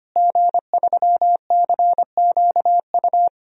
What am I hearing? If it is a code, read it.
Morse code, heard as a keyed tone, about 25 wpm, 700 Hz. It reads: G3CQU
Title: DXCC-100 Call Signs - 5 Characters - Repeated 2x - 25wpm